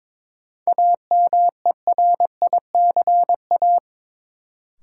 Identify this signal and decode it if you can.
Morse code, heard as a keyed tone, about 22 wpm, 700 Hz. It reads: AMERICA
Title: Sets of 5 Words - New Word Drill - 22wpm